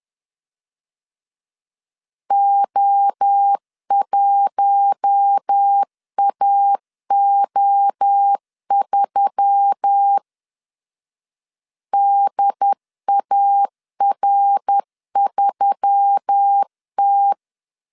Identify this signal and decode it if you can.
Morse code, heard as a keyed tone, about 10 wpm, 800 Hz. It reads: O1AO3 DAR3T
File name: CW / DET / 1 6 1500C S B 10wpm 4137